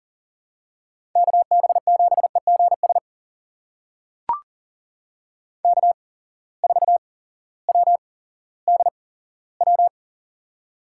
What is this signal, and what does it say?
Morse code, heard as a keyed tone, about 39 wpm, spaced as for 18 wpm, 700 Hz. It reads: KB7EGS K4WDW
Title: USA Call Signs - 8x Character Spacing - 40wpm